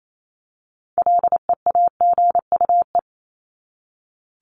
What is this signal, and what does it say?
Morse code, heard as a keyed tone, about 28 wpm, 700 Hz. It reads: LEAGUE
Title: Top 1751-2000 Words 28wpm